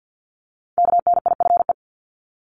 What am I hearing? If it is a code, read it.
Morse code, heard as a keyed tone, about 50 wpm, 700 Hz. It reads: KNIFE
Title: Multi-letter Phonemes 50wpm